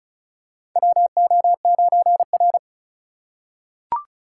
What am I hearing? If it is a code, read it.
Morse code, heard as a keyed tone, about 34 wpm, 690 Hz. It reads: WO9R
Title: USA Call Signs - 4 Characters - 2x Character Spacing - 35wpm